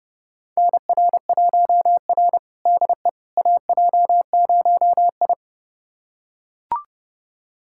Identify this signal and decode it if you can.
Morse code, heard as a keyed tone, about 32 wpm, 700 Hz. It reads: NR1R DE AJ0I
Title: USA Callsign DE Callsign - 4 Characters - 30wpm